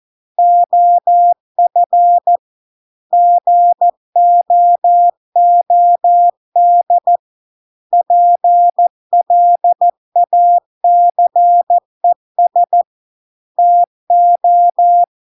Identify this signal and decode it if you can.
Morse code, heard as a keyed tone, about 14 wpm, 700 Hz. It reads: OF GOOD PLACES TO